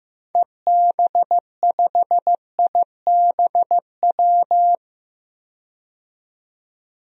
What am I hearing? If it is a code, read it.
Morse code, heard as a keyed tone, about 15 wpm, 700 Hz. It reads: EB5IBW
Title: DXCC-100 Call Signs - 6 Characters - 15wpm